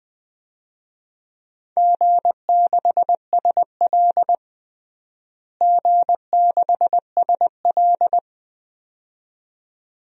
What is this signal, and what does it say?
Morse code, heard as a keyed tone, about 20 wpm, 700 Hz. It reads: G6SL G6SL
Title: DXCC-100 Call Signs - 4 Characters - Repeated 2x - 20wpm